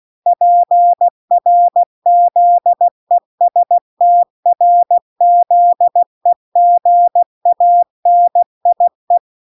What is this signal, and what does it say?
Morse code, heard as a keyed tone, about 16 wpm, 700 Hz. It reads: PRZESTRZEGANIE